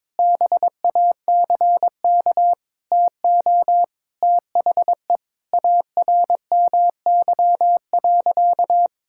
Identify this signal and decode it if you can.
Morse code, heard as a keyed tone, about 22 wpm, 700 Hz. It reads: BACK TO THE ARMY.
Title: Sentences from Top 1000 Words - Review Part 6 - 22wpm